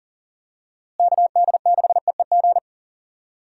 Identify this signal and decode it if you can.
Morse code, heard as a keyed tone, about 40 wpm, 700 Hz. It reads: KD6EEG
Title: USA Call Signs - 6 Characters - Repeated 2x - 40wpm